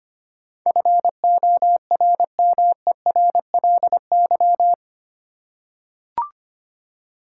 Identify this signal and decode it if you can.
Morse code, heard as a keyed tone, about 25 wpm, 700 Hz. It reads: FORMERLY
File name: Eight Letter Words - 25wpm